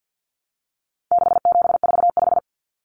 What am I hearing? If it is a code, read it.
Morse code, heard as a keyed tone, about 50 wpm, 700 Hz. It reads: 6745
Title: Common Cut Numbers - 4-Digit 50wpm